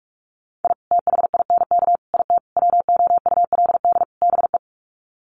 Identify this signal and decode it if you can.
Morse code, heard as a keyed tone, about 44 wpm, 700 Hz. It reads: I THINK IT WOULD BE
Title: Sets of 5 Words 45wpm